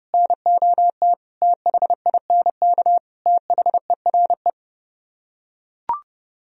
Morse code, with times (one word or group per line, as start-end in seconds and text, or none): 0.1-1.1 NOT
1.4-3.0 THINK
3.3-4.5 THERE